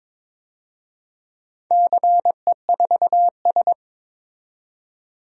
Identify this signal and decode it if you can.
Morse code, heard as a keyed tone, about 22 wpm, 700 Hz. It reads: CE4S
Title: DXCC-100 Call Signs - 4 Characters - 22wpm